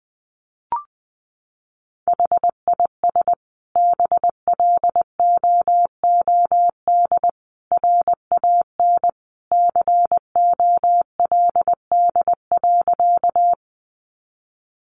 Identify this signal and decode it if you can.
Morse code, heard as a keyed tone, about 20 wpm, 700 Hz. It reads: HIS BLOOD RAN COLD.
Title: Sentences from Top 1100 Words 20wpm